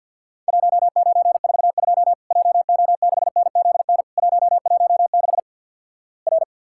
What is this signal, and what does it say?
Morse code, heard as a keyed tone, about 50 wpm, 700 Hz. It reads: 1942 JO6NZN 116 R
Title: Worked All Europe DX Contest - QTC Exchange - 50wpm